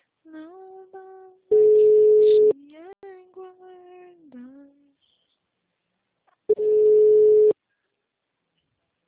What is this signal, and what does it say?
Ringback tone (Europe, 425 Hz), with speech.